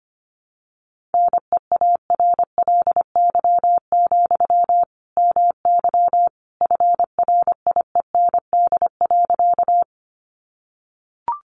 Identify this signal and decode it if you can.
Morse code, heard as a keyed tone, about 25 wpm, 700 Hz. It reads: NEARLY, MY FRIEND.
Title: Sentences from Top 1300 Words 25wpm